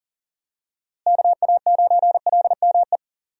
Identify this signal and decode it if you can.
Morse code, heard as a keyed tone, about 40 wpm, 700 Hz. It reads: KA9LME
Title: USA Call Signs - 6 Characters - Repeated 3x - 40wpm